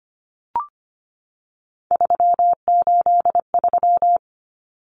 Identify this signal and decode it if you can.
Morse code, heard as a keyed tone, about 25 wpm, 690 Hz. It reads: 383
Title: Numbers - 3-Digit 25wpm